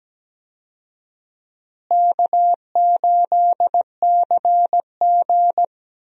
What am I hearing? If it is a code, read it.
Morse code, heard as a keyed tone, about 17 wpm, 700 Hz. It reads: K8CG
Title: USA Call Signs - 4 Characters - Repeated 3x - 17wpm